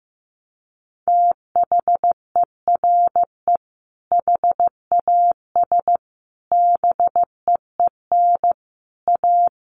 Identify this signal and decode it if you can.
Morse code, heard as a keyed tone, about 15 wpm, 700 Hz. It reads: THERE HAS BEEN A
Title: Sentences from Top 1500 Words 15wpm